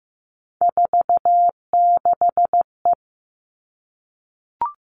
Morse code, heard as a keyed tone, about 15 wpm, 700 Hz. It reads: 46E
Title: All Cut Numbers - 3-Digit 15wpm